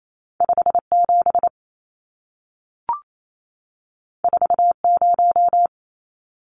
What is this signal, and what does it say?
Morse code, heard as a keyed tone, about 28 wpm, 700 Hz. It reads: 57 40
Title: Numbers - 2-Digit 28wpm